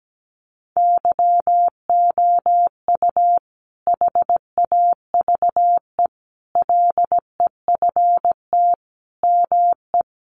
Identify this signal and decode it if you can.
Morse code, heard as a keyed tone, about 17 wpm, 700 Hz. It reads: YOU HAVE LEFT ME
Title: Sentences from Top 600 Words 17wpm